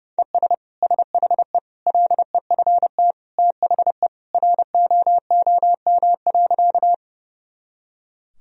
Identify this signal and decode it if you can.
Morse code, heard as a keyed tone, about 30 wpm, 700 Hz. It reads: ES SHE LEFT THE ROOM.
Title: Sentences from Top 300 Words and Abbreviations 30wpm